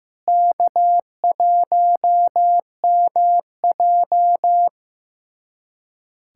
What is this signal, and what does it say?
Morse code, heard as a keyed tone, about 15 wpm, 700 Hz. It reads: K1MJ
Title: USA Callsign DE Callsign 15wpm